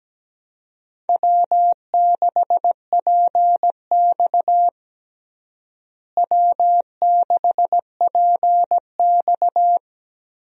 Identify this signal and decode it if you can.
Morse code, heard as a keyed tone, about 17 wpm, 700 Hz. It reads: W6PX W6PX
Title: USA Call Signs - 4 Characters - Repeated 3x - 17wpm